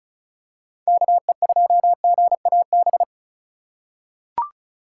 Morse code, heard as a keyed tone, about 34 wpm, 700 Hz. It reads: KE2GAB
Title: USA Call Signs - 6 Characters - 3x Character Spacing - 35wpm